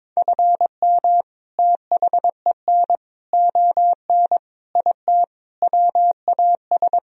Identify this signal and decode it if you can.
Morse code, heard as a keyed tone, about 22 wpm, 700 Hz. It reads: FM THEN ON IT WAS
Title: Sentences from Top 400 Words and Abbreviations 22wpm